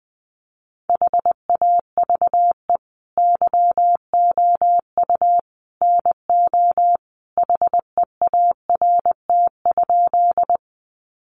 Morse code, heard as a keyed tone, about 20 wpm, 700 Hz. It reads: HAVE YOU NO HEART?
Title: Sentences from Top 1200 Words 20wpm